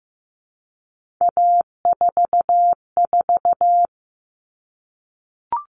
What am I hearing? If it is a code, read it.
Morse code, heard as a keyed tone, about 15 wpm, 700 Hz. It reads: A44
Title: All Cut Numbers - 3-Digit 15wpm